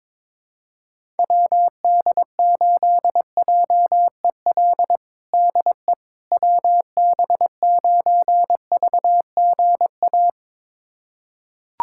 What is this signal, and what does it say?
Morse code, heard as a keyed tone, about 21 wpm, 700 Hz. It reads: WD8JEL DE WB9VGA
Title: USA Callsign DE Callsign - 6 Characters - 22wpm